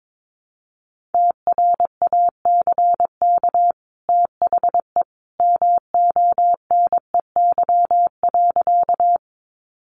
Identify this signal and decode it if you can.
Morse code, heard as a keyed tone, about 22 wpm, 700 Hz. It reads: TRACK THE MONEY.
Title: Sentences from Top 1100 Words 22wpm